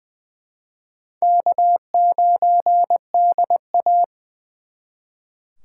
Morse code, heard as a keyed tone, about 25 wpm, 690 Hz. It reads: K9DA
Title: Speed Racing - USA 1x2 Call Signs 20wpm